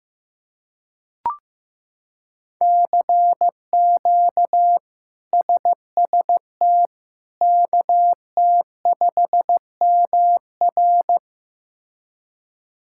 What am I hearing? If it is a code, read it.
Morse code, heard as a keyed tone, about 15 wpm, 700 Hz. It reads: CQ SST KT5MR
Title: K1USN SST QSO Practice - 15wpm